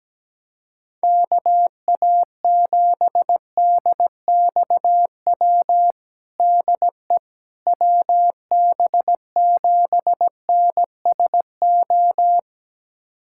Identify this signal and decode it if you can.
Morse code, heard as a keyed tone, about 17 wpm, 700 Hz. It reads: KA7DXW DE WB7NSO
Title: USA Callsign DE Callsign 17wpm